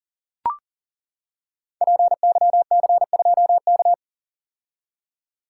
Morse code, heard as a keyed tone, about 40 wpm, 700 Hz. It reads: PYC2K
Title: Code Groups - 5 Letter:Numbers - 40wpm